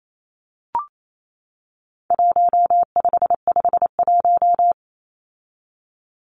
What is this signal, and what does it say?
Morse code, heard as a keyed tone, about 28 wpm, 700 Hz. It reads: 1551
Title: Numbers - Contest Style - 4-Digit 28wpm